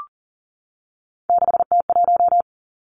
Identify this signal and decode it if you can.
Morse code, heard as a keyed tone, about 40 wpm, 700 Hz. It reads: 6T1
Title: Three Challenging Letter-Number Combinations 40wpm